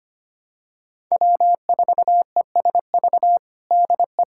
Morse code, heard as a keyed tone, about 25 wpm, 700 Hz. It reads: W4ESV DE
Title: USA Callsign DE Callsign - 5 Characters - 25wpm